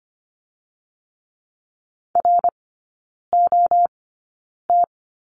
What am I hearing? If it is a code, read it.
Morse code, heard as a keyed tone, about 25 wpm, spaced as for 9 wpm, 700 Hz. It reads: ROT
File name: Lesson 14e - Number 2 - Three Characters 25wpm